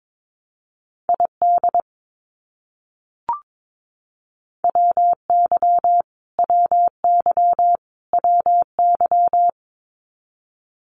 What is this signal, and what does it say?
Morse code, heard as a keyed tone, about 22 wpm, 700 Hz. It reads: ID WY WY WY
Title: US States Repeated 3x 22wpm